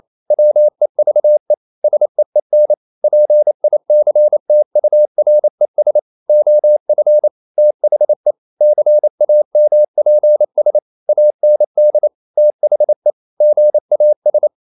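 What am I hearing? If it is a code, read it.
Morse code, heard as a keyed tone, about 28 wpm, 600 Hz. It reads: WEVE SEEN PICTURES OF THE CAMPS AND THE GAS
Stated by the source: Morse code practice at 28 wpm, text of Ken Follett's novel Never in English, from CW med Gnister.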